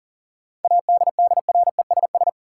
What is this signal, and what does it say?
Morse code, heard as a keyed tone, about 40 wpm, 700 Hz. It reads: ADDRESS
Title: Sentences from Top 1000 Words - New Word Drill - 40wpm